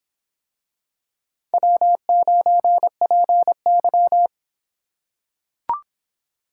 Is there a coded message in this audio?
Yes: Morse code from 1.5 to 4.3 s, heard as a keyed tone, about 24 wpm, 700 Hz. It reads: W9PY